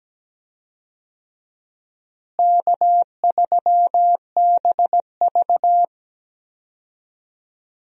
Morse code, heard as a keyed tone, about 17 wpm, 700 Hz. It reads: K3BV
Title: USA Call Signs - 4 Characters - Repeated 3x - 17wpm